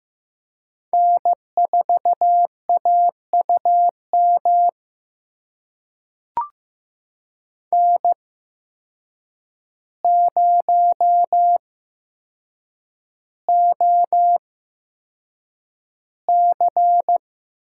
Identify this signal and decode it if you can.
Morse code, heard as a keyed tone, about 15 wpm, spaced as for 7 wpm, 700 Hz. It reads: N4AUM N0OC
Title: USA Call Signs - 5 Characters - 8x Character Spacing - 15wpm